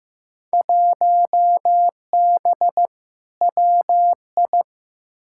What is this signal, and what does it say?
Morse code, heard as a keyed tone, about 15 wpm, 700 Hz. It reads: 1B WI
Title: Field Day Exchanges - 15wpm